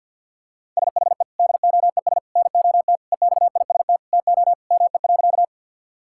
Morse code, heard as a keyed tone, about 50 wpm, 700 Hz. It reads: SHE DOES NOT EXIST TO ME.
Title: Sentences from Top 1000 Words - Review Part 5 - 50wpm